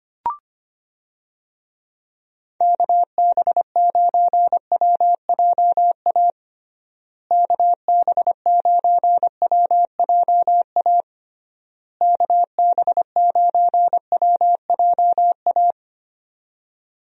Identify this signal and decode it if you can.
Morse code, heard as a keyed tone, about 25 wpm, 700 Hz. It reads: KB9WJA KB9WJA KB9WJA
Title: USA Call Signs - 6 Characters - Repeated 3x - 25wpm